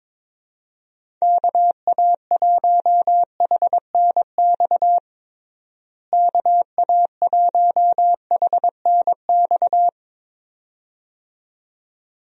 Morse code, heard as a keyed tone, about 22 wpm, 700 Hz. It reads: KA1HNX KA1HNX
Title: USA Call Signs - 6 Characters - Repeated 2x - 22wpm